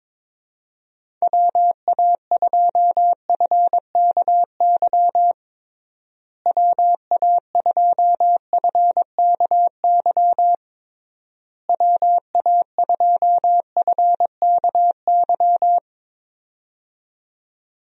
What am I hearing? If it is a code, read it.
Morse code, heard as a keyed tone, about 21 wpm, 700 Hz. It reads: WA2FKY WA2FKY WA2FKY
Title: USA Call Signs - 6 Characters - Repeated 3x - 22wpm